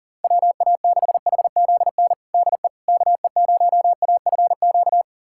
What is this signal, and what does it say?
Morse code, heard as a keyed tone, about 39 wpm, 700 Hz. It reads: WA6HZN DE KE0AFQ
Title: USA Callsign DE Callsign - 6 Characters - 40wpm